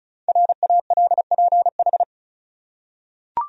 Morse code, heard as a keyed tone, about 34 wpm, 700 Hz. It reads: RALPH
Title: USA Common Names - 35wpm